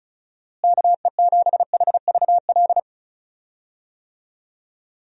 Morse code, heard as a keyed tone, about 34 wpm, 700 Hz. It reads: KE7HVL
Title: USA Call Signs - 6 Characters - Repeated 2x - 35wpm